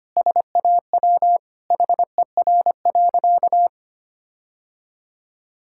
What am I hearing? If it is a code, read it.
Morse code, heard as a keyed tone, about 25 wpm, 700 Hz. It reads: SAW HER.
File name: practice / 25wpm / Sentences with 4-Word Phrases - 25wpm